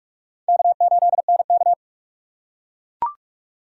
Morse code, heard as a keyed tone, about 43 wpm, 700 Hz. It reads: K8NK
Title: USA Call Signs - 4x Character Spacing - 45wpm